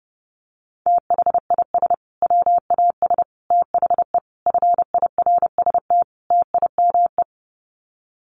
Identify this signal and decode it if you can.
Morse code, heard as a keyed tone, about 30 wpm, 700 Hz. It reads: THIS WAS THE FIRST TIME
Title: Sets of 5 Words Encore 30wpm